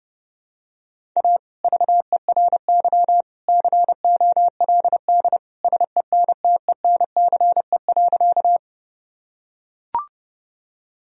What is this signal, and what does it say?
Morse code, heard as a keyed tone, about 32 wpm, 700 Hz. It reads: A VERY COLD SENTENCE.
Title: Sentences from Top 1000 Words - Review Part 1 - 30wpm